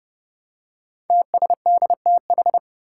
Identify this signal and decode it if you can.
Morse code, heard as a keyed tone, about 30 wpm, 700 Hz. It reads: TSDTH
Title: Five Challenging Letter-Number Combinations 30wpm